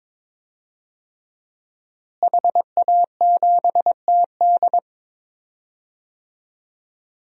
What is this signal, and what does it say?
Morse code, heard as a keyed tone, about 22 wpm, 700 Hz. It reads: HA7TD
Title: DXCC-100 Call Signs - 5 Characters - Repeated 3x - 22wpm